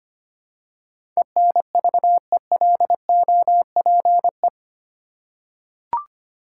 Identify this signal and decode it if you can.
Morse code, heard as a keyed tone, about 25 wpm, 690 Hz. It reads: ENVELOPE
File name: 8 Letter Words - First 3 Letters Given - 25wpm